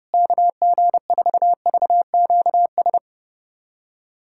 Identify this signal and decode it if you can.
Morse code, heard as a keyed tone, about 30 wpm, 700 Hz. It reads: KG4VQS
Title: USA Callsign DE Callsign 30wpm